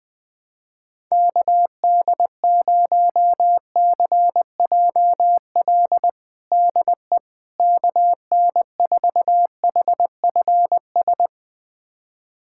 Morse code, heard as a keyed tone, about 20 wpm, 700 Hz. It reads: KD0CJL DE KN4HFS
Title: USA Callsign DE Callsign 20wpm